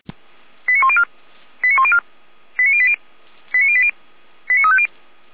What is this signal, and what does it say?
Five-tone selective call (ZVEI), heard as quick runs of five tones, tones of about 70 ms. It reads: 89104 89104 89080 89080 89350